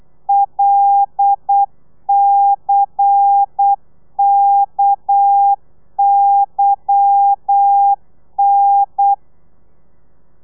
Morse code, heard as a keyed tone, about 8 wpm, 800 Hz. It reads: LCKYN
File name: telegrafia / i2tfj / Lez10 35